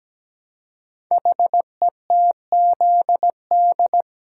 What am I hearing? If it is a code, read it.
Morse code, heard as a keyed tone, about 17 wpm, 700 Hz. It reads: HETZD
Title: Code Groups - 5 Letters - 17wpm